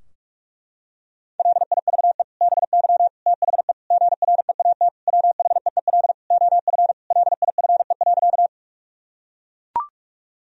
Morse code, heard as a keyed tone, about 42 wpm, 700 Hz. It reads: LIVE BY THE GREAT WHEEL OF LIFE.
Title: Sentences from Top 1000 Words - Review Part 3 - 45wpm